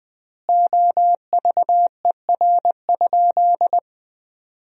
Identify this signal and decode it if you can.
Morse code, heard as a keyed tone, about 21 wpm, 700 Hz. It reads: OVER?
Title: 4x Word-Spacing - Sentences from Top 100 Words - 20wpm